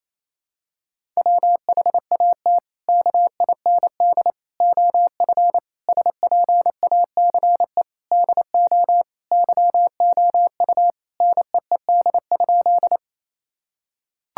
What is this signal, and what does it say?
Morse code, heard as a keyed tone, about 28 wpm, 700 Hz. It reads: WHAT KIND OF SPACE DO YOU NEED?
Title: Sentences from Top 1000 Words - Review Part 5 - 28wpm